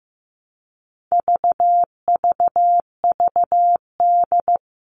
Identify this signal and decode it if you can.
Morse code, heard as a keyed tone, about 15 wpm, 700 Hz. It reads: VVVD
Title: All Cut Numbers - 4-Digit 15wpm